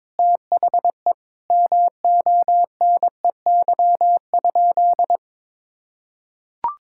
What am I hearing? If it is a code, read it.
Morse code, heard as a keyed tone, about 22 wpm, 700 Hz. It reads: THE MONEY?
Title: Sentences from Top 1000 Words - Review Part 6 - 22wpm